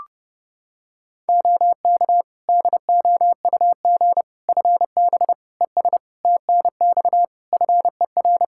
Morse code, heard as a keyed tone, about 32 wpm, 700 Hz. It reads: OK DOUG FB ES TNX FER
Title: Practice QSO - Exchange 2 - Info - 30wpm